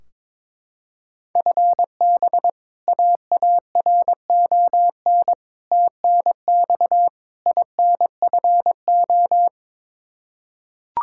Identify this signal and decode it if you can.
Morse code, heard as a keyed tone, about 21 wpm, 700 Hz. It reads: FB AARON TNX INFO
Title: Practice QSO - Exchange 3 - More Info - 22wpm